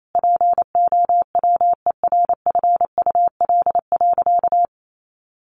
Morse code, heard as a keyed tone, about 28 wpm, 700 Hz. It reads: POWERFUL.
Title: Sentences from Top 1300 Words 28wpm